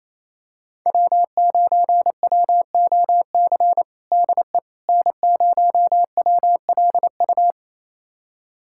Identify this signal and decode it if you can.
Morse code, heard as a keyed tone, about 28 wpm, 700 Hz. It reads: W9WOC DE N0WLU
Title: USA Callsign DE Callsign - 5 Characters - 28wpm